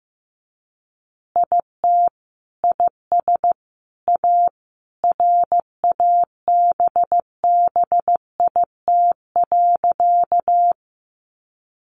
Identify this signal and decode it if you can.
Morse code, heard as a keyed tone, about 15 wpm, 700 Hz. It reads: IT IS A RABBIT.
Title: Easy Phrases 15wpm